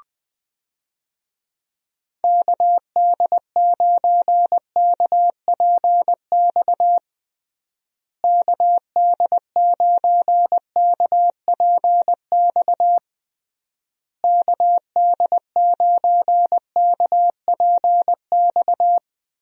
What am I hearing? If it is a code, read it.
Morse code, heard as a keyed tone, about 20 wpm, 700 Hz. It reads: KD9KPX KD9KPX KD9KPX
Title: USA Call Signs - 6 Characters - Repeated 3x - 20wpm